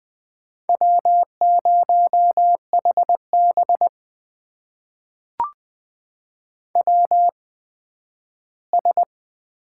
Morse code, heard as a keyed tone, about 20 wpm, spaced as for 12 wpm, 700 Hz. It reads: W0HB WS
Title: USA Call Signs - 8x Character Spacing - 20wpm